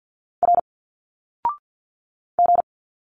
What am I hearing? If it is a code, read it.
Morse code, heard as a keyed tone, about 50 wpm, 700 Hz. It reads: R G
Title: Single Letters 50wpm